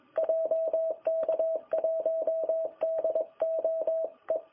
Morse code, heard as a keyed tone, about 20 wpm, 660 Hz. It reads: JK1DOE
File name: jh8cbh / 2021cw